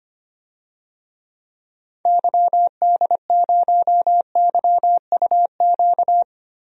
Morse code, heard as a keyed tone, about 25 wpm, 700 Hz. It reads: YD0YUQ
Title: DXCC-100 Call Signs - 6 Characters - Repeated 2x - 25wpm